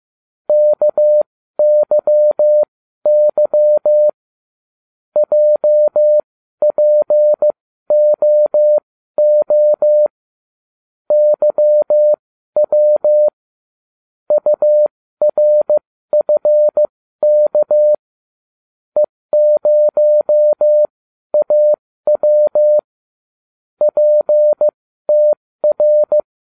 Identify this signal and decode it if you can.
Morse code, heard as a keyed tone, about 15 wpm, spaced as for 12 wpm, 600 Hz. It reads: KYY JPOO YW URFK E0AW PTR